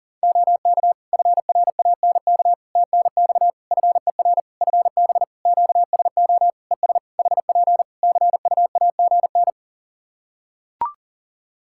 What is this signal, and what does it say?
Morse code, heard as a keyed tone, about 40 wpm, 700 Hz. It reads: OK FRANK TNX FER FB QSO ES HP CUAGN
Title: Practice QSO - Exchange 4 - Ending - 40wpm